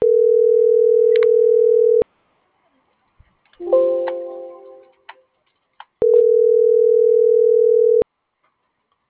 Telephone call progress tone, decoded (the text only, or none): ringback tone